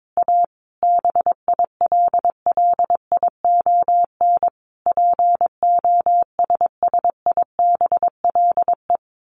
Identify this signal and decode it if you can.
Morse code, heard as a keyed tone, about 22 wpm, 700 Hz. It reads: A BILLION POSSIBLE
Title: A Sound Of Thunder 22wpm